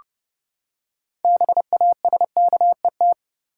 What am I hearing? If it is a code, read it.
Morse code, heard as a keyed tone, about 30 wpm, 700 Hz. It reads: BASKET
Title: Six Letter Words - 30wpm